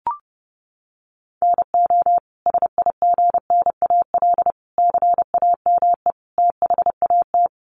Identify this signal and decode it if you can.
Morse code, heard as a keyed tone, about 30 wpm, 700 Hz. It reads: NO SIGNAL CAME THAT
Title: Sentences from Top 1400 Words 30wpm